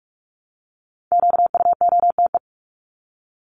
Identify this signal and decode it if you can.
Morse code, heard as a keyed tone, about 44 wpm, 700 Hz. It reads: QUOTE
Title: Top 1751-2000 Words 45wpm